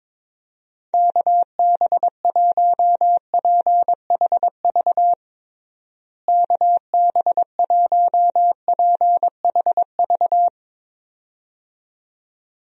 Morse code, heard as a keyed tone, about 22 wpm, 700 Hz. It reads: KB1PHV KB1PHV
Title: USA Call Signs - 6 Characters - Repeated 2x - 22wpm